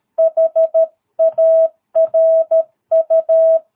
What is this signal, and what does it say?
Morse code, heard as a keyed tone, about 13 wpm, 650 Hz. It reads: HARU
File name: jh8cbh / 2022cw